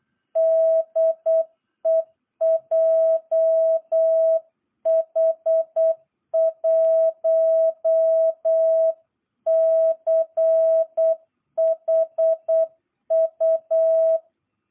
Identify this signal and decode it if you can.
Morse code, heard as a keyed tone, about 8 wpm, 660 Hz. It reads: DE JH1CHU
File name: jh8cbh / 2022cw